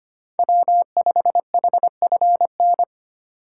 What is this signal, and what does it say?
Morse code, heard as a keyed tone, about 25 wpm, 700 Hz. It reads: W5HFN
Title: USA Call Signs - 5 Characters - 3x Character Spacing - 25wpm